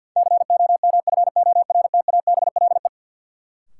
Morse code, heard as a keyed tone, about 52 wpm, 700 Hz. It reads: COMFORTABLE
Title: Sentences from Top 1500 Words - New Word Drill Alt - 50wpm